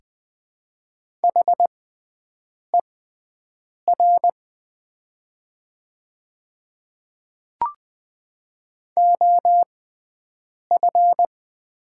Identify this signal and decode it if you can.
Morse code, heard as a keyed tone, about 20 wpm, spaced as for 7 wpm, 690 Hz. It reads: HER OF
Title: 6x Character Spacing - Top 100 Words - 20wpm